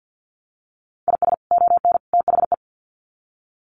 Morse code, heard as a keyed tone, about 48 wpm, 700 Hz. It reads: IS ON THE
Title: Sets of 3 Words 50wpm